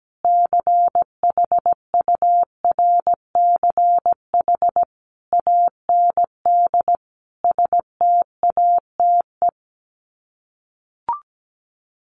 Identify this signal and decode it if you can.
Morse code, heard as a keyed tone, about 17 wpm, 700 Hz. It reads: CHURCH AND STATE
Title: Sets of 5 Words 17wpm